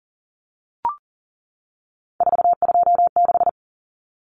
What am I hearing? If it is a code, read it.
Morse code, heard as a keyed tone, about 40 wpm, 700 Hz. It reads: 426
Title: Common Cut Numbers - 3-Digit 40wpm